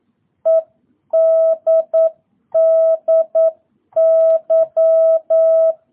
Morse code, heard as a keyed tone, about 9 wpm, 660 Hz. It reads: EDDY